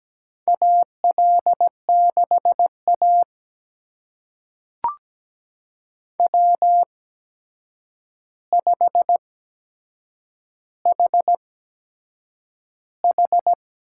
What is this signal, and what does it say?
Morse code, heard as a keyed tone, about 17 wpm, spaced as for 7 wpm, 700 Hz. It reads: AL6A W5HH
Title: USA Call Signs - 4 Characters - 8x Character Spacing - 17wpm